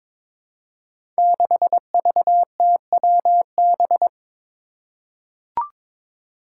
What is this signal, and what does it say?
Morse code, heard as a keyed tone, about 22 wpm, 700 Hz. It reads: 6VTWB
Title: Five Challenging Letter-Number Combinations 22wpm